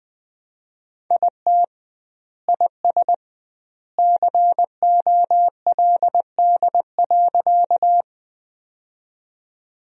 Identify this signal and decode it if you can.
Morse code, heard as a keyed tone, about 21 wpm, 700 Hz. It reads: IT IS COLD.
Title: 2x Word-Spacing - Sentences from Top 400 Words - 20wpm